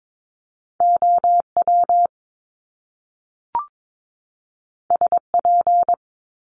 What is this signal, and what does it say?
Morse code, heard as a keyed tone, about 22 wpm, 700 Hz. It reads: OW SP
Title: Top Two Letter Combinations in Words 22wpm